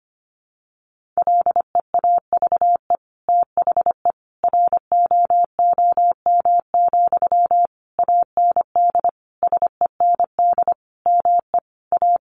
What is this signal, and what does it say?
Morse code, heard as a keyed tone, about 25 wpm, 700 Hz. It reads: LEAVE THE ROOM, AND SEND ME A
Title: Sentences from Top 1500 Words 25wpm